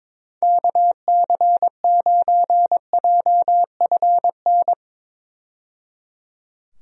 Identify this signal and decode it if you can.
Morse code, heard as a keyed tone, about 22 wpm, 700 Hz. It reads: KC9JFN
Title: USA Call Signs - 6 Characters - 22wpm